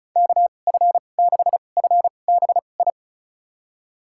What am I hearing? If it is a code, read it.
Morse code, heard as a keyed tone, about 36 wpm, spaced as for 25 wpm, 700 Hz. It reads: KF6FBI
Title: USA Call Signs - 6 Characters - 2x Character Spacing - 35wpm